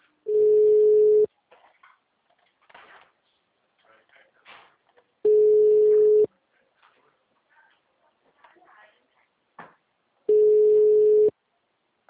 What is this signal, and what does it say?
Ringback tone (Europe, 425 Hz), with speech.
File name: out- -2044- - - .9903